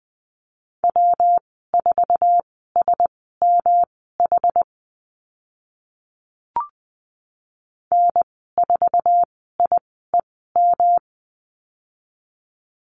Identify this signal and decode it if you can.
Morse code, heard as a keyed tone, about 20 wpm, spaced as for 14 wpm, 700 Hz. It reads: W4SMH N4IEM
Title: 122 Lesson 20g - Number 3 - Call Signs 20wpm